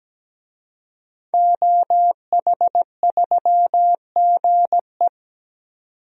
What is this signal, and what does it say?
Morse code, heard as a keyed tone, about 17 wpm, 700 Hz. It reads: OH3GE
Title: DXCC-100 Call Signs - 5 Characters - 17wpm